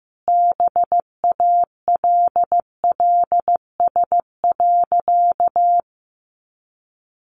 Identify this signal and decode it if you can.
Morse code, heard as a keyed tone, about 15 wpm, 700 Hz. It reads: BALLS.
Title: Easy Phrases 15wpm